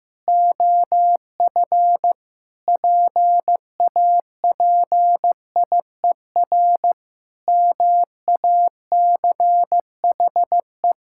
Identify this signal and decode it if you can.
Morse code, heard as a keyed tone, about 15 wpm, 700 Hz. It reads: OF PAPIER MACHE
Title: Pangram Challenge v2 - 15wpm